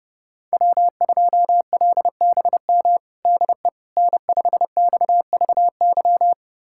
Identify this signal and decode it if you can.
Morse code, heard as a keyed tone, about 30 wpm, 700 Hz. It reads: W2LBM DE N5XVY
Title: USA Callsign DE Callsign 30wpm